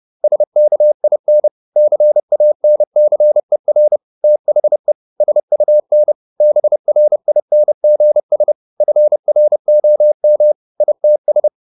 Morse code, heard as a keyed tone, about 30 wpm, 600 Hz. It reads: SKIN CANCER THE SUN BRINGS FROM ITS
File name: the old man and the sea / the old man and the sea 1 30wpm